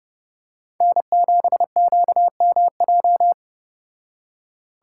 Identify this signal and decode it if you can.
Morse code, heard as a keyed tone, about 29 wpm, 700 Hz. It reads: N7QMJ